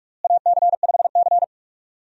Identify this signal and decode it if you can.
Morse code, heard as a keyed tone, about 46 wpm, 700 Hz. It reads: AC5C
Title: USA Call Signs - 4 Characters - Repeated 2x - 45wpm